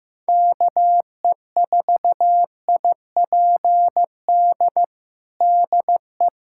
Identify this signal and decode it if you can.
Morse code, heard as a keyed tone, about 15 wpm, 700 Hz. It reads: KE4IPD DE
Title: USA Callsign DE Callsign 15wpm